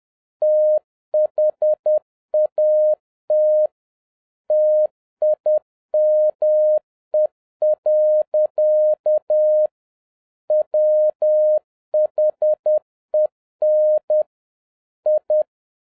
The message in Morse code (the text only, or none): THAT TIME. WHEN I